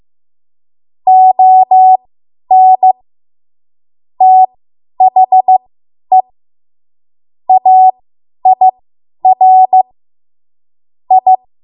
Morse code, heard as a keyed tone, about 15 wpm, spaced as for 10 wpm, 750 Hz. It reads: ON THE AIR I
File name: Morse / 10WPM